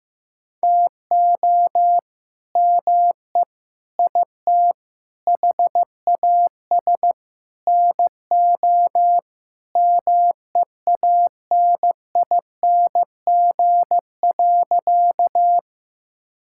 Morse code, heard as a keyed tone, about 15 wpm, 700 Hz. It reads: TO ME IT HAS NO MEANING.
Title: Sentences from Top 1000 Words - Review Part 6 - 15wpm